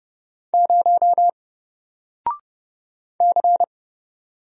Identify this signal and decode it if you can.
Morse code, heard as a keyed tone, about 31 wpm, 700 Hz. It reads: 0 C
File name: Single-Letter-Number-30wpm